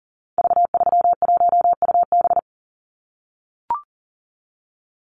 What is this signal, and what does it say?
Morse code, heard as a keyed tone, about 40 wpm, 700 Hz. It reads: V31UB
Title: Call Signs - 40wpm